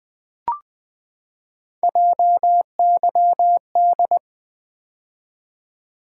Morse code, heard as a keyed tone, about 20 wpm, 700 Hz. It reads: JYD
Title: Three Letter-Number Combinations - 20wpm